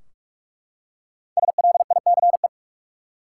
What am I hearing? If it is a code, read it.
Morse code, heard as a keyed tone, about 41 wpm, 700 Hz. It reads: SLICE
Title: Five Letter Words - 45wpm